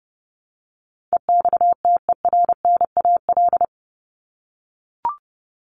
Morse code, heard as a keyed tone, about 30 wpm, 700 Hz. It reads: EXTERNAL
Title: Top 1751-2000 Words 30wpm